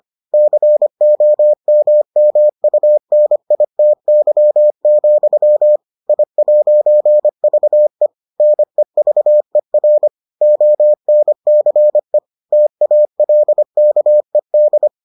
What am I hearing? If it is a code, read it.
Morse code, heard as a keyed tone, about 25 wpm, 600 Hz. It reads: COMMUNITY, I'VE NEVER ONCE TALKED